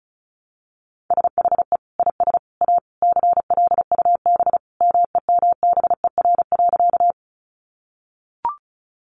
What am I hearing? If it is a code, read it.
Morse code, heard as a keyed tone, about 34 wpm, 700 Hz. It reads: SHE IS A CLUB MEMBER.
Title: Sentences from Top 1200 Words 35wpm